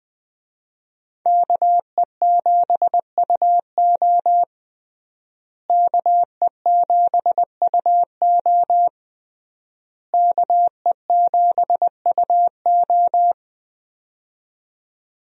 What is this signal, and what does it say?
Morse code, heard as a keyed tone, about 20 wpm, 700 Hz. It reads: KE7UO KE7UO KE7UO
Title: USA Call Signs - 5 Characters - Repeated 3x - 20wpm